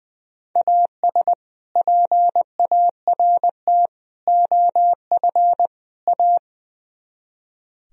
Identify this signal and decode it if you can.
Morse code, heard as a keyed tone, about 22 wpm, 700 Hz. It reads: AS PART OF A
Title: Speed-Racing - Sets of 4 Words 20wpm